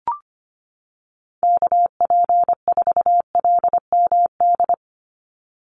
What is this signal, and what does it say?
Morse code, heard as a keyed tone, about 25 wpm, 700 Hz. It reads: KP4LMD
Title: USA Call Signs 25wpm